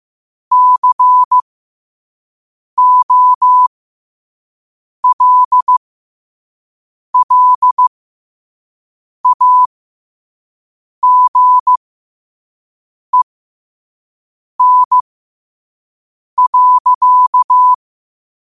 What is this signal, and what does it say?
Morse code, heard as a keyed tone, about 15 wpm, spaced as for 5 wpm, 1000 Hz. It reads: COLLAGEN.
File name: qotdfiles / 5wpm